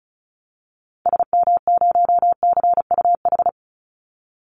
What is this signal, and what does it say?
Morse code, heard as a keyed tone, about 34 wpm, 700 Hz. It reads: SM0CUH
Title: Call Signs Repeated 3x - 35wpm